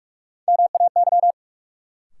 Morse code, heard as a keyed tone, about 44 wpm, 700 Hz. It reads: MAY
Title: Sentences from Top 100 Words - New Word Drill - 45wpm